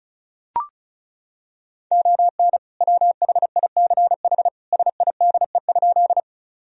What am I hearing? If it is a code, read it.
Morse code, heard as a keyed tone, about 36 wpm, 700 Hz. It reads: ON WHICH SIDE?
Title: Sentences from Top 1000 Words - Review Part 1 - 35wpm